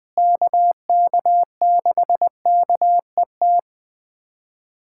Morse code, heard as a keyed tone, about 20 wpm, 700 Hz. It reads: KK6KET
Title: USA Callsign DE Callsign - 6 Characters - 20wpm